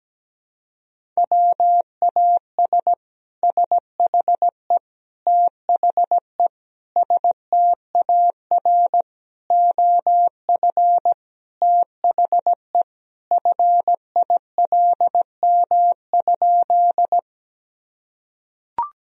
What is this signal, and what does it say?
Morse code, heard as a keyed tone, about 17 wpm, 700 Hz. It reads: WAS SHE THE STAR OF THE FILM?
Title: Sentences from Top 1000 Words - Review Part 3 - 17wpm